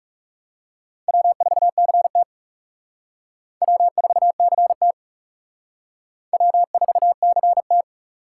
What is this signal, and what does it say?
Morse code, heard as a keyed tone, about 38 wpm, 700 Hz. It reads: W4CT W4CT W4CT